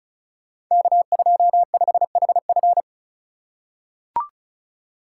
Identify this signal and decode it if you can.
Morse code, heard as a keyed tone, about 36 wpm, 700 Hz. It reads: K25HF
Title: Code Groups - 5 Letter:Numbers - 35wpm